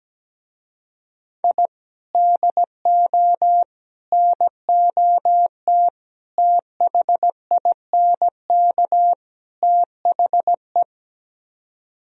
Morse code, heard as a keyed tone, about 17 wpm, 700 Hz. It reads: I DO NOT THINK THE
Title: Speed-Racing - Sets of 5 Words 17wpm